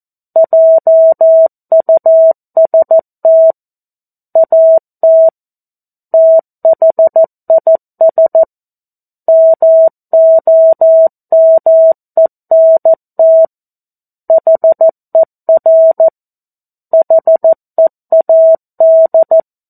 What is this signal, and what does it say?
Morse code, heard as a keyed tone, about 14 wpm, 650 Hz. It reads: JUST AT THIS MOMENT HER HEAD
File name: 02-Chapter-02 14 WPM